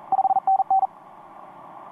Morse code, heard as a keyed tone, about 38 wpm, 770 Hz. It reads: 5NN
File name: t46rrc i5fln etc 15m cw 18feb2013 1248